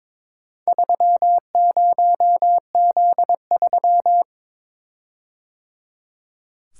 Morse code, heard as a keyed tone, about 22 wpm, 700 Hz. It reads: 30Z3
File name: practice / 22wpm / Four Letter-Number Combinations - 22wpm